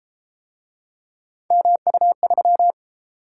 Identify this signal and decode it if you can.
Morse code, heard as a keyed tone, about 31 wpm, 700 Hz. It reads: MU3